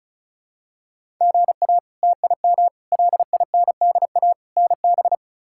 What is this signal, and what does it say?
Morse code, heard as a keyed tone, about 36 wpm, 700 Hz. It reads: GA TIM LINDA NB
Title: K1USN SST QSO Practice - 35wpm